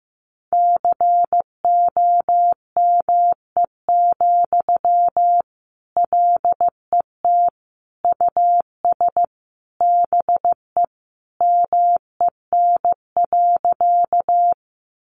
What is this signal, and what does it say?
Morse code, heard as a keyed tone, about 15 wpm, 700 Hz. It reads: COME, LET US BE MEN.
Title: Sentences from Top 300 Words 15wpm